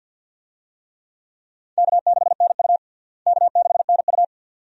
Morse code, heard as a keyed tone, about 52 wpm, 700 Hz. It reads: K6NU K6NU
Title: USA Call Signs - 4 Characters - Repeated 2x - 50wpm